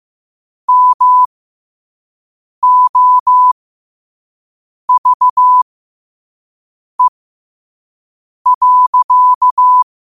Morse code, heard as a keyed tone, about 15 wpm, spaced as for 5 wpm, 1000 Hz. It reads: MOVE.
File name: qotdfiles / 5wpm